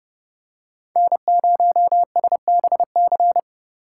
Morse code, heard as a keyed tone, about 30 wpm, 700 Hz. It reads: N0SBC
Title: USA Call Signs - 5 Characters - 3x Character Spacing - 30wpm